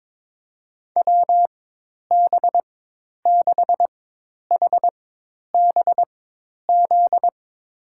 Morse code, heard as a keyed tone, about 22 wpm, spaced as for 10 wpm, 700 Hz. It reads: WB6HBZ
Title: USA Call Signs - 6 Characters - 4x Character Spacing - 22wpm